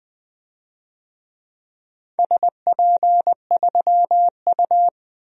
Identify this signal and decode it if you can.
Morse code, heard as a keyed tone, about 20 wpm, 700 Hz. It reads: SP3U
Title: DXCC-100 Call Signs - 4 Characters - Repeated 2x - 20wpm